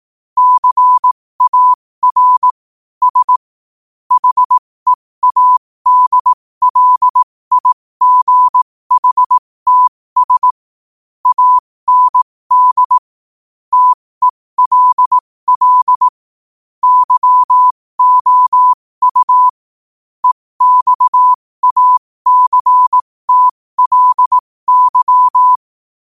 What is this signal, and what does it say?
Morse code, heard as a keyed tone, about 18 wpm, spaced as for 15 wpm, 1000 Hz. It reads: CAR S HEADLIGHTS AND TELL YOU EXACTLY
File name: qotdfiles / 15wpm